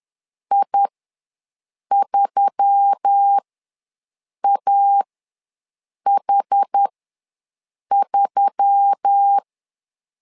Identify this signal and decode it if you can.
Morse code, heard as a keyed tone, about 10 wpm, spaced as for 6 wpm, 800 Hz. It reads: I3AH3
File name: LEZIONI / 4 Q1S3 ws 10wpm 2437